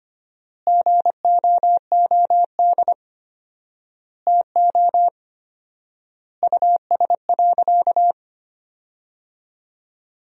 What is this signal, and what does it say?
Morse code, heard as a keyed tone, about 25 wpm, 700 Hz. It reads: GOOD TO US.
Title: 4x Word-Spacing - Sentences from Top 200 Words - 25wpm